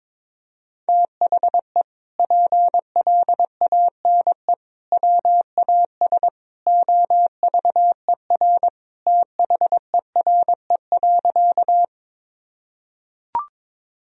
Morse code, heard as a keyed tone, about 21 wpm, 700 Hz. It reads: THE PLANE WAS OVER THERE.
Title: Sentences from Top 1000 Words - Review Part 3 - 22wpm